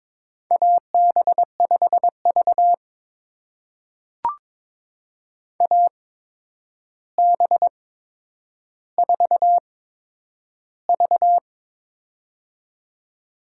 Morse code, heard as a keyed tone, about 21 wpm, spaced as for 9 wpm, 700 Hz. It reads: AB5V AB4V
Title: USA Call Signs - 4 Characters - 8x Character Spacing - 22wpm